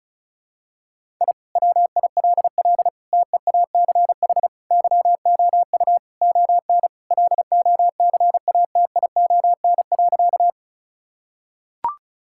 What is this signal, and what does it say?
Morse code, heard as a keyed tone, about 36 wpm, 700 Hz. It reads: I WILL TEACH YOU ON LOCATION.
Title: Sentences from Top 1000 Words - Review Part 6 - 35wpm